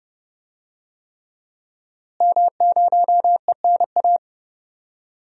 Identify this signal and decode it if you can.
Morse code, heard as a keyed tone, about 32 wpm, 700 Hz. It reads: M0ENA
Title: DXCC-100 Call Signs - 5 Characters - Repeated 3x - 30wpm